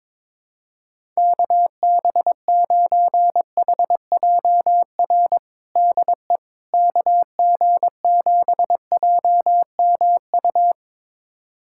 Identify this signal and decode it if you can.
Morse code, heard as a keyed tone, about 22 wpm, 700 Hz. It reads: KB9HJR DE KG7JMU
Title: USA Callsign DE Callsign - 6 Characters - 22wpm